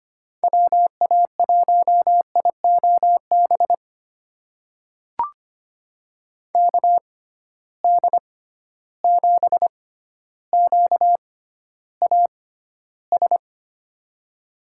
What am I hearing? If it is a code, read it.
Morse code, heard as a keyed tone, about 25 wpm, spaced as for 13 wpm, 700 Hz. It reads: WA1IOB KD7QAS
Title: USA Call Signs - 6 Characters - 6x Character Spacing - 25wpm